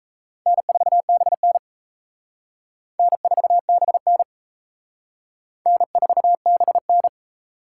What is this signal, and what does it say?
Morse code, heard as a keyed tone, about 35 wpm, 700 Hz. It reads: N4BN N4BN N4BN
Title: Speed Racing - USA 1x2 Call Signs 28wpm